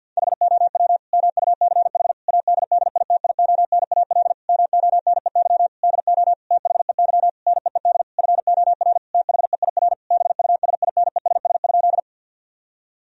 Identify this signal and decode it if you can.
Morse code, heard as a keyed tone, about 50 wpm, 700 Hz. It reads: HOW MUCH ADDITIONAL MONEY DO THEY NEED FOR THEIR BUSINESS?
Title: Sentences from Top 1000 Words - Review Part 4 - 50wpm